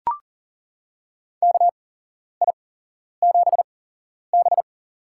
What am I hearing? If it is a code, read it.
Morse code, heard as a keyed tone, about 40 wpm, spaced as for 11 wpm, 700 Hz. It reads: KI7B
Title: USA Call Signs - 4 Characters - 8x Character Spacing - 40wpm